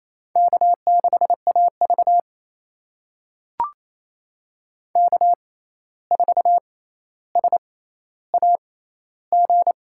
Morse code, heard as a keyed tone, about 28 wpm, spaced as for 13 wpm, 700 Hz. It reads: K6AV K4SAG
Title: USA Call Signs - 6x Character Spacing - 28wpm